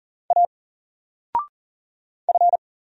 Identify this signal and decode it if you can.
Morse code, heard as a keyed tone, about 39 wpm, 700 Hz. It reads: A F
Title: Single Letter-Number 40wpm